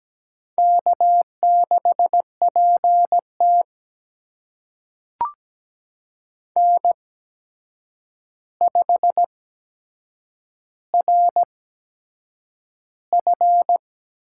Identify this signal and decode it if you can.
Morse code, heard as a keyed tone, about 17 wpm, spaced as for 7 wpm, 700 Hz. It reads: K6PT N5RF
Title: USA Call Signs - 8x Character Spacing - 17wpm